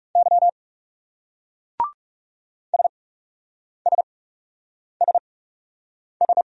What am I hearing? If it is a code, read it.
Morse code, heard as a keyed tone, about 39 wpm, spaced as for 8 wpm, 700 Hz. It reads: Y SSSS